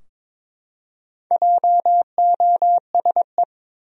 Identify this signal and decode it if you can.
Morse code, heard as a keyed tone, about 22 wpm, 700 Hz. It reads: JOSE
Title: USA Common Names - Max 5 Letters - 22wpm